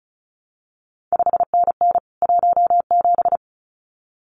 Morse code, heard as a keyed tone, about 36 wpm, 700 Hz. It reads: 5NN 17
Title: CQ WW DX Contest Exchange Encore - 35wpm